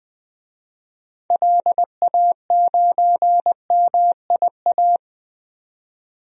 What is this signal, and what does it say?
Morse code, heard as a keyed tone, about 20 wpm, 700 Hz. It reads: LA9MIA
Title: DXCC-100 Call Signs - 6 Characters - 20wpm